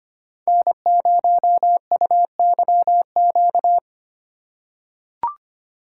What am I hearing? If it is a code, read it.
Morse code, heard as a keyed tone, about 25 wpm, 700 Hz. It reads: N0UYQ
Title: USA Call Signs - 5 Characters - 4x Character Spacing - 25wpm